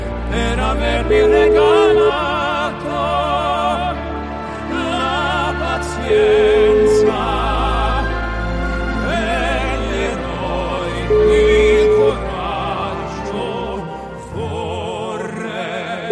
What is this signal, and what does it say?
Ringback tone (Europe, 425 Hz), with noise.